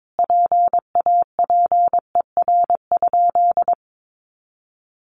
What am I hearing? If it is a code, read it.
Morse code, heard as a keyed tone, about 22 wpm, 700 Hz. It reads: PAPER?
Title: Sentences from Top 1300 Words 22wpm